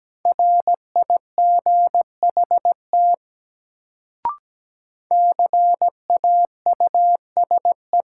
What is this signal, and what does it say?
Morse code, heard as a keyed tone, about 17 wpm, 700 Hz. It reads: RIGHT CAUSE
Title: Sentences from Top 200 Words - New Word Drill - 17wpm